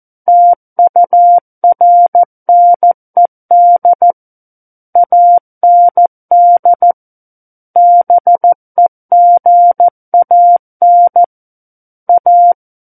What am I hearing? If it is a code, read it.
Morse code, heard as a keyed tone, about 14 wpm, 700 Hz. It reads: TURNED AND BEGAN A
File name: Book One / 05-Chapter 5 - 14 WPM